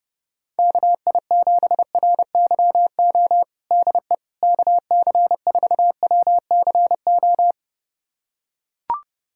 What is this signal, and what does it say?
Morse code, heard as a keyed tone, about 32 wpm, 700 Hz. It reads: KI7RYO DE KC4WCO
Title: USA Callsign DE Callsign - 6 Characters - 30wpm